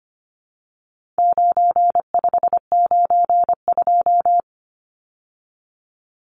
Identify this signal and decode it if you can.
Morse code, heard as a keyed tone, about 25 wpm, 700 Hz. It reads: 9592
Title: Numbers - 4-Digit 25wpm